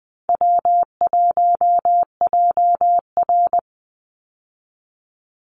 Morse code, heard as a keyed tone, about 20 wpm, 700 Hz. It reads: W1JR
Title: Call Signs - 20wpm